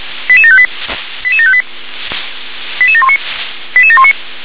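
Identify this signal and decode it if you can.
Five-tone selective call (ZVEI), heard as quick runs of five tones, tones of about 70 ms. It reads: 88758 88758 88719 88719
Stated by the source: ZVEI five-tone selective call received on an SDR.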